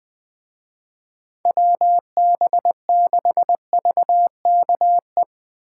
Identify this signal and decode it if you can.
Morse code, heard as a keyed tone, about 22 wpm, 700 Hz. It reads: WB6VKE
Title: Speed Racing - USA 2x3 Call Signs 20wpm